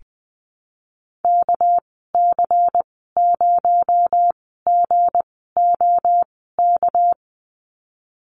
Morse code, heard as a keyed tone, about 20 wpm, spaced as for 15 wpm, 700 Hz. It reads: KC0GOK
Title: 231 Lesson 37g - Letter x - Call Signs 20wpm